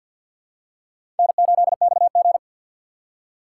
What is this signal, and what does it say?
Morse code, heard as a keyed tone, about 50 wpm, 700 Hz. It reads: N8XG
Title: USA Call Signs - 4 Characters - 2x Character Spacing - 50wpm